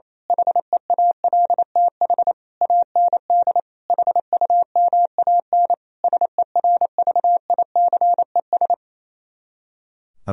Morse code, heard as a keyed tone, about 27 wpm, 700 Hz. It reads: HEALTH AND HUMAN SERVICES
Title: Speed-Racing - Sets of 5 Words 28wpm